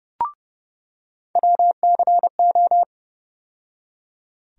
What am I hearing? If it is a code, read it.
Morse code, heard as a keyed tone, about 30 wpm, 700 Hz. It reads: WCO
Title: USA Call Sign Postfixes - 3 Letters 30wpm